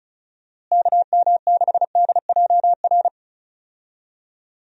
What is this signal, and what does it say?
Morse code, heard as a keyed tone, about 36 wpm, 700 Hz. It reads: KM6DJR
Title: USA Call Signs - 6 Characters - Repeated 3x - 35wpm